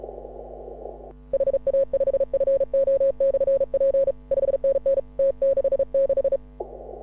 Morse code, heard as a keyed tone, about 36 wpm, 540 Hz. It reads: HA5FO/P 5NN T66
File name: AUDIO2018 / HA6OI CW